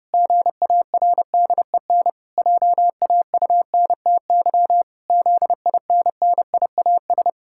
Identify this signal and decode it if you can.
Morse code, heard as a keyed tone, about 30 wpm, 700 Hz. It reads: GARDEN JAUNTY ZINNIAS
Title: Pangram Challenge v2 - 30wpm